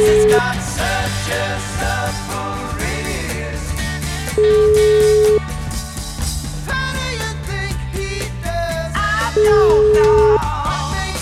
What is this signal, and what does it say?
Ringback tone (Europe, 425 Hz), with noise.